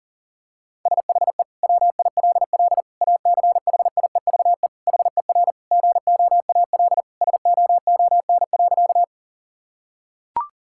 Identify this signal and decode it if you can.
Morse code, heard as a keyed tone, about 40 wpm, 700 Hz. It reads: SHE WILL ACHIEVE HER GOAL SOON.
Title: Sentences from Top 1000 Words - Review Part 6 - 40wpm